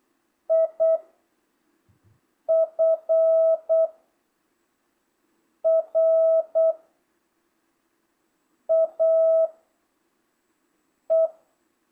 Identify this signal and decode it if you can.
Morse code, heard as a keyed tone, about 8 wpm, 660 Hz. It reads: I F R A E